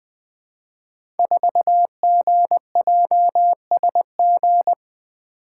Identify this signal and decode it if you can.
Morse code, heard as a keyed tone, about 20 wpm, 700 Hz. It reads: 4GJSG
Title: Code Groups - 5 Letter:Numbers - 20wpm